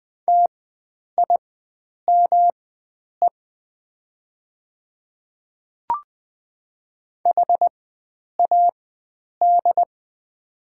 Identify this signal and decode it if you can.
Morse code, heard as a keyed tone, about 20 wpm, spaced as for 9 wpm, 700 Hz. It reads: TIME HAD
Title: 4x Character Spacing - Top 100 Words - 20wpm